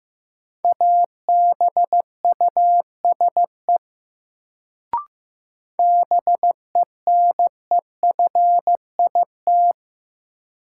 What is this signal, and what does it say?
Morse code, heard as a keyed tone, about 15 wpm, 700 Hz. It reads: ABUSE BENEFIT
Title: Sentences from Top 1300 Words - New Word Drill Alt - 15wpm